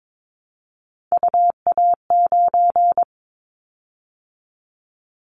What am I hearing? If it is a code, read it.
Morse code, heard as a keyed tone, about 22 wpm, 700 Hz. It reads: UA9
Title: Callsign Prefixes for Top 50 DXCC - 22wpm